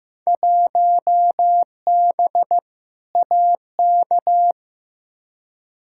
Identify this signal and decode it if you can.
Morse code, heard as a keyed tone, about 16 wpm, 700 Hz. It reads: 1B AK
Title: Field Day Exchanges Encore - 15wpm